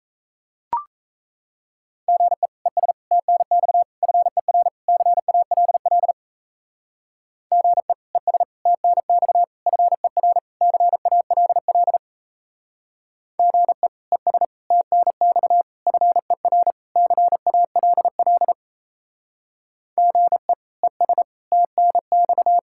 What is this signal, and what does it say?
Morse code, heard as a keyed tone, about 34 wpm, 700 Hz. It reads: GE ES TNX FER CALL GE ES TNX FER CALL GE ES TNX FER CALL GE ES TNX
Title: Speed-Racing Common QSO Phrases Encore - 28wpm